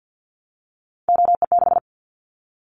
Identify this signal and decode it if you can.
Morse code, heard as a keyed tone, about 50 wpm, 700 Hz. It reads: OE6
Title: Callsign Prefixes for Top 50 DXCC - 50wpm